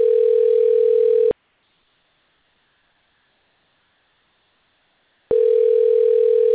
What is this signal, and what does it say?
Ringback tone (North America), 440 + 480 Hz.
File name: recordings / RE7271d434f34a103a7763d82b8b1dcf43